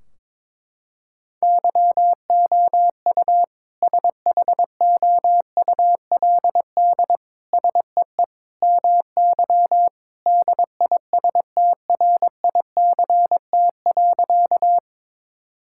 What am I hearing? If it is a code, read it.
Morse code, heard as a keyed tone, about 22 wpm, 700 Hz. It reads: YOU SHOULD SEE MY DISTRICT.
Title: Sentences from Top 1600 Words 22wpm